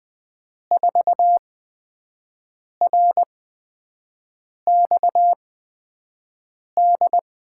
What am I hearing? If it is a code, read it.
Morse code, heard as a keyed tone, about 20 wpm, spaced as for 5 wpm, 700 Hz. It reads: 4RXD
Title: USA Call Signs - 5 Characters - 8x Character Spacing - 20wpm